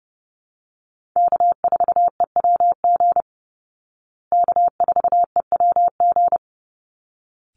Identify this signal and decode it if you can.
Morse code, heard as a keyed tone, about 30 wpm, 700 Hz. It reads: K4EWG K4EWG
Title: Call Signs Repeated 3x - 30wpm